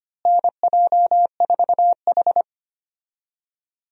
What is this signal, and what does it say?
Morse code, heard as a keyed tone, about 25 wpm, 700 Hz. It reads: NJ4H
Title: USA Call Signs - 4 Characters - Repeated 2x - 25wpm